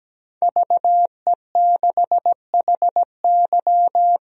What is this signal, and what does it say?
Morse code, heard as a keyed tone, about 17 wpm, 700 Hz. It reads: VE6HY
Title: DXCC-100 Call Signs - 5 Characters - 17wpm